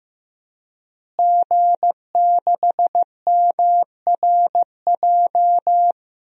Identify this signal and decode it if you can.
Morse code, heard as a keyed tone, about 15 wpm, 700 Hz. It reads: G6MRJ
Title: DXCC-100 Call Signs - 5 Characters - Repeated 3x - 15wpm